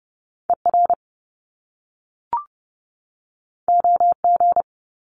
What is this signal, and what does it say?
Morse code, heard as a keyed tone, about 30 wpm, 700 Hz. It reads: ER OG
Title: Top Two Letter Combinations in Words 30wpm